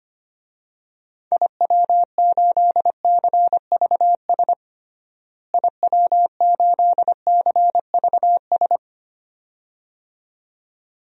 Morse code, heard as a keyed tone, about 25 wpm, 700 Hz. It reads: IW8CVS IW8CVS
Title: DXCC-100 Call Signs - 6 Characters - Repeated 2x - 25wpm